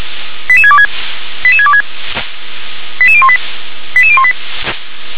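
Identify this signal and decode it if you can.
Five-tone selective call (ZVEI), heard as quick runs of five tones, tones of about 70 ms. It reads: 88626 88626 88018 88018
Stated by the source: ZVEI five-tone selective call 88626 received with an SDR.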